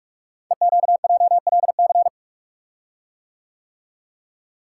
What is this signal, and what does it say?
Morse code, heard as a keyed tone, about 44 wpm, 700 Hz. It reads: EQJLC
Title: Code Groups - 5 Letter:Numbers - 45wpm